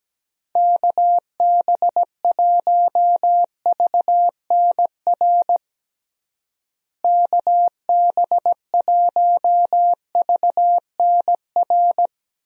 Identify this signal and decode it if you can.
Morse code, heard as a keyed tone, about 17 wpm, 700 Hz. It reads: KB1VNR KB1VNR
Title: USA Call Signs - 6 Characters - Repeated 3x - 17wpm